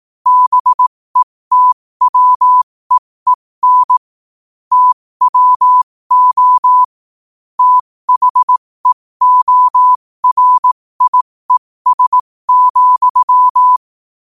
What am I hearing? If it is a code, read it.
Morse code, heard as a keyed tone, about 18 wpm, spaced as for 15 wpm, 1000 Hz. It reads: BETWEEN TWO THEORIES,
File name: qotdfiles / 15wpm